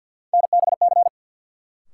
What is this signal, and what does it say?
Morse code, heard as a keyed tone, about 52 wpm, 700 Hz. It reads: NBC
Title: Sets of 5 Words Encore - New Word Drill - 50wpm